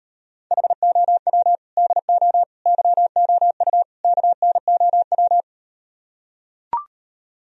Morse code, heard as a keyed tone, about 38 wpm, 690 Hz. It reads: HOW DO YOU KNOW